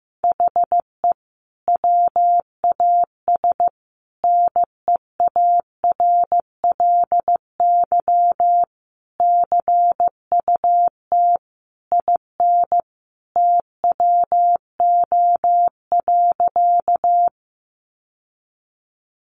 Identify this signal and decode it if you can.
Morse code, heard as a keyed tone, about 15 wpm, 700 Hz. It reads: HE WAS NEARLY CUT IN TWO.
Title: Sentences from Top 1300 Words 15wpm